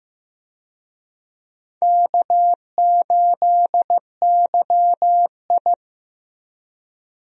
Morse code, heard as a keyed tone, about 15 wpm, 700 Hz. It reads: K8YI
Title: USA Call Signs - 4 Characters - 15wpm